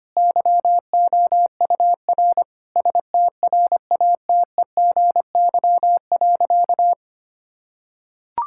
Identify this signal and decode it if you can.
Morse code, heard as a keyed tone, about 25 wpm, 700 Hz. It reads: YOUR STRATEGY.
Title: Sentences from Top 1000 Words - Review Part 5 - 25wpm